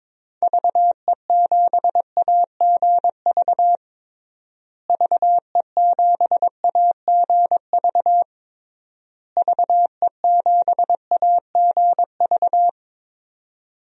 Morse code, heard as a keyed tone, about 22 wpm, 700 Hz. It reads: VE7AGV VE7AGV VE7AGV
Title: DXCC-100 Call Signs - 6 Characters - Repeated 3x - 22wpm